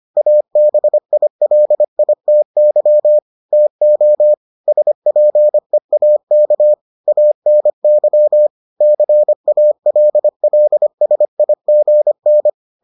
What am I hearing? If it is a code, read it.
Morse code, heard as a keyed tone, about 25 wpm, 600 Hz. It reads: ABILITY TO SPEAK ANY CALLSIGN